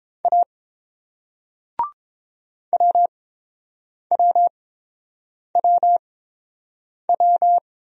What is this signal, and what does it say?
Morse code, heard as a keyed tone, about 29 wpm, spaced as for 7 wpm, 690 Hz. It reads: A WWWW